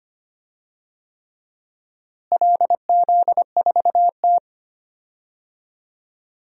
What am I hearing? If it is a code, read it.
Morse code, heard as a keyed tone, about 25 wpm, 700 Hz. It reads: LZ4T
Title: DXCC-100 Call Signs - 4 Characters - Repeated 3x - 25wpm